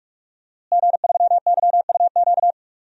Morse code, heard as a keyed tone, about 41 wpm, 700 Hz. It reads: G3YUQ
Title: DXCC-100 Call Signs - 5 Characters - Repeated 3x - 45wpm